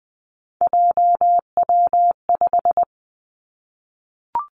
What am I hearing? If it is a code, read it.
Morse code, heard as a keyed tone, about 20 wpm, 700 Hz. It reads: JW5
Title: Three Challenging Letter-Number Combinations 20wpm